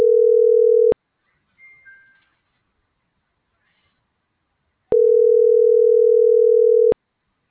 Ringback tone (North America), 440 + 480 Hz.